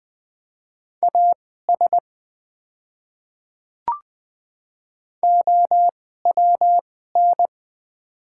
Morse code, heard as a keyed tone, about 21 wpm, spaced as for 15 wpm, 700 Hz. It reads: AS OWN
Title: 2x Character Spacing - Top 300 Words - 20wpm